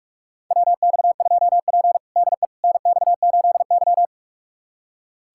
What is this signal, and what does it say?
Morse code, heard as a keyed tone, about 44 wpm, 700 Hz. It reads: WX2P DE NX8Y
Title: USA Callsign DE Callsign - 4 Characters - 45wpm